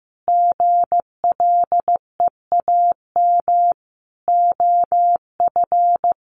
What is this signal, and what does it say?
Morse code, heard as a keyed tone, about 15 wpm, 700 Hz. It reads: GLEAM OF
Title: A Sound Of Thunder 15wpm